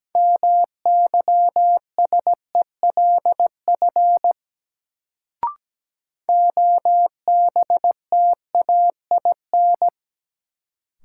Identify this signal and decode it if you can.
Morse code, heard as a keyed tone, about 17 wpm, 700 Hz. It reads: MYSELF OBTAIN
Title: Sentences from Top 1300 Words - New Word Drill - 17wpm